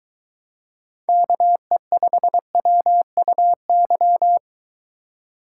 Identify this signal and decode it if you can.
Morse code, heard as a keyed tone, about 22 wpm, 700 Hz. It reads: KE5WUY